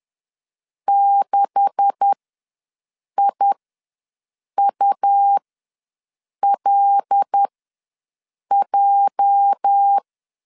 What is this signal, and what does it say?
Morse code, heard as a keyed tone, about 10 wpm, spaced as for 6 wpm, 800 Hz. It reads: 6IULJ
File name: LEZIONI / 9 286 ws 10wpm 4032